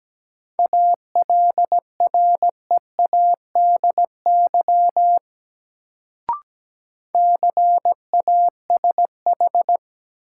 Morse code, heard as a keyed tone, about 17 wpm, 700 Hz. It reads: ALREADY CASH
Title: Sentences from Top 700 Words - New Word Drill - 17wpm